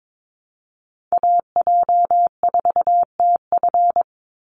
Morse code, heard as a keyed tone, about 22 wpm, 700 Hz. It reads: AJ4TF
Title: USA Call Signs 22wpm